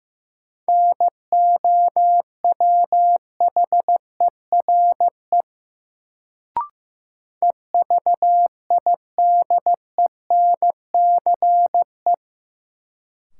Morse code, heard as a keyed tone, about 15 wpm, 700 Hz. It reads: NOWHERE EVIDENCE
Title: Sets of 5 Words Encore - New Word Drill - 15wpm